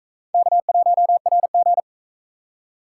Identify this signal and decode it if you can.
Morse code, heard as a keyed tone, about 37 wpm, 700 Hz. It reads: K1RG